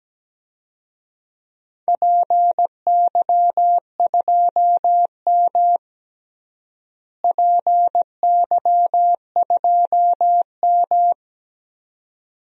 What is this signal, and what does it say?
Morse code, heard as a keyed tone, about 17 wpm, 700 Hz. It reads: PY2M PY2M
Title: DXCC-100 Call Signs - 4 Characters - Repeated 2x - 17wpm